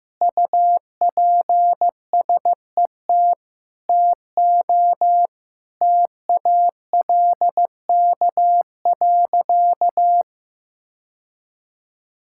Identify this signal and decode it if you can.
Morse code, heard as a keyed tone, about 15 wpm, 700 Hz. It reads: UPSET TO TALK.
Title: Sentences from Top 1000 Words - Review Part 5 - 15wpm